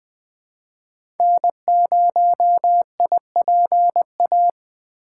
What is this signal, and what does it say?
Morse code, heard as a keyed tone, about 20 wpm, 700 Hz. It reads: N0IPA
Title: 2023 SOTA Chaser Call Signs - 20wpm